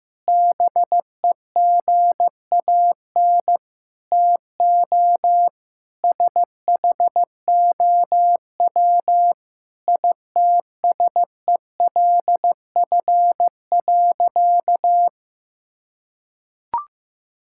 Morse code, heard as a keyed tone, about 15 wpm, 700 Hz. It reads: BEGAN TO SHOW ITSELF.
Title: Sentences from Top 1000 Words - Review Part 5 - 15wpm